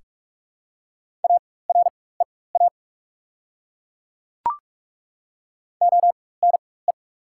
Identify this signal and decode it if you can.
Morse code, heard as a keyed tone, about 43 wpm, spaced as for 21 wpm, 690 Hz. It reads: AREA ONE
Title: 4x Character Spacing - Top 500 Words - 45wpm